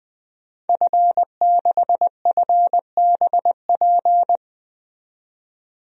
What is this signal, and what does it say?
Morse code, heard as a keyed tone, about 21 wpm, 700 Hz. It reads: F6FBP
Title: DXCC-100 Call Signs - 5 Characters - 20wpm